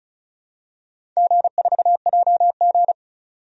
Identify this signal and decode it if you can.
Morse code, heard as a keyed tone, about 36 wpm, 700 Hz. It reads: G4JG
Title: DXCC-100 Call Signs - 4 Characters - 35wpm